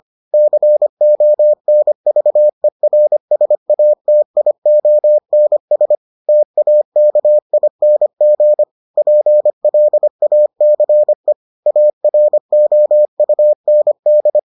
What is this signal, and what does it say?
Morse code, heard as a keyed tone, about 25 wpm, 600 Hz. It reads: CONVERSATIONS TAKING PLACE AROUND